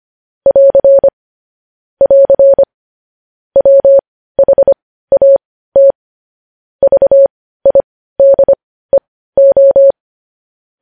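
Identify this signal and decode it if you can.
Morse code, heard as a keyed tone, about 25 wpm, spaced as for 15 wpm, 550 Hz. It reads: <AR> <AR> WHAT VIDEO
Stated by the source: News headlines sent in Morse code at 15 words per minute, preceded by vvv.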